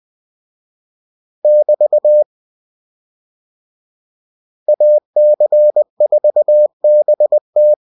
Morse code, heard as a keyed tone, about 20 wpm, 600 Hz. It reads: <BT> AC4BT